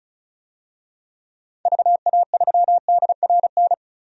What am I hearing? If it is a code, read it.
Morse code, heard as a keyed tone, about 36 wpm, 690 Hz. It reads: VA3DRN
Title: DXCC-100 Call Signs - 6 Characters - 35wpm